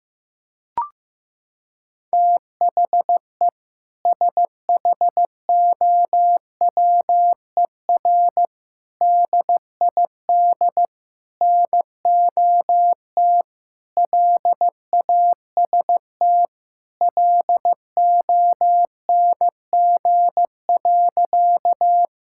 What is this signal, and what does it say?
Morse code, heard as a keyed tone, about 15 wpm, 700 Hz. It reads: THE SHOWER DID NOT LAST LONG.
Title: Sentences from Top 1600 Words 15wpm